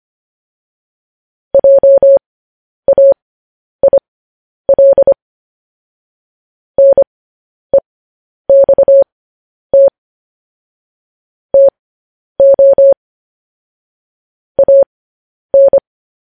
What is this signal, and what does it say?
Morse code, heard as a keyed tone, about 25 wpm, spaced as for 10 wpm, 550 Hz. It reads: JAIL NEXT TO AN
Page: news headlines in morse code at 10 wpm